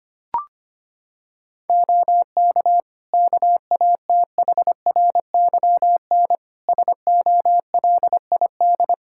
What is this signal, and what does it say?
Morse code, heard as a keyed tone, about 25 wpm, 700 Hz. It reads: OK KATHRYN SOLID
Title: K1USN SST QSO Practice 25wpm